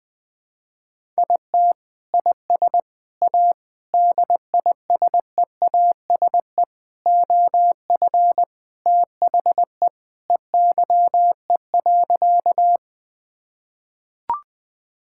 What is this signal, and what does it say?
Morse code, heard as a keyed tone, about 20 wpm, 700 Hz. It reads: IT IS A DISEASE OF THE EYE.
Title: Sentences from Top 1000 Words - Review Part 6 - 20wpm